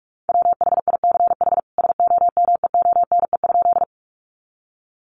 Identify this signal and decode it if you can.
Morse code, heard as a keyed tone, about 44 wpm, 700 Hz. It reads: WHICH SOMEONE?
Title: Sentences from Top 600 Words 45wpm